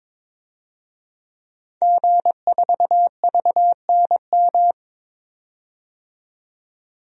Morse code, heard as a keyed tone, about 21 wpm, 700 Hz. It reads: G4VNM
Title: DXCC-100 Call Signs - 5 Characters - 22wpm